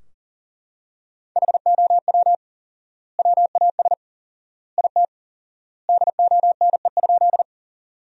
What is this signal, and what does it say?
Morse code, heard as a keyed tone, about 40 wpm, 700 Hz. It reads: HOW WAS IT DONE?
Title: 4x Word-Spacing - Sentences from Top 500 Words - 40wpm